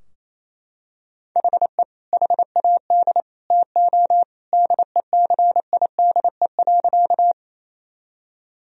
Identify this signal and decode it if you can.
Morse code, heard as a keyed tone, about 28 wpm, 700 Hz. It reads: HE HAD TO DECIDE.
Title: Sentences from Top 1000 Words - Review Part 3 - 28wpm